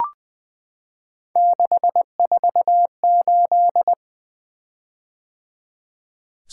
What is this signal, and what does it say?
Morse code, heard as a keyed tone, about 20 wpm, 700 Hz. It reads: 648
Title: Three Letter-Number Combinations - 20wpm